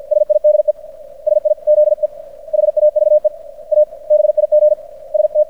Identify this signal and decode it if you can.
Morse code, heard as a keyed tone, about 50 wpm, 600 Hz. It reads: SIDE SIZE HAVE A BIG ST
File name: archived / CRSnet